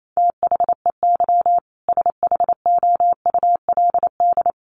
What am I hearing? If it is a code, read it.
Morse code, heard as a keyed tone, about 28 wpm, 700 Hz. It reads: THEY SHOULD
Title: Sentences from Top 1500 Words 28wpm